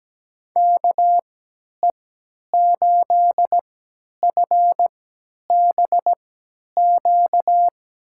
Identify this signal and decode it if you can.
Morse code, heard as a keyed tone, about 17 wpm, spaced as for 10 wpm, 700 Hz. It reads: KE8FBQ
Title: USA Call Signs - 6 Characters - 3x Character Spacing - 17wpm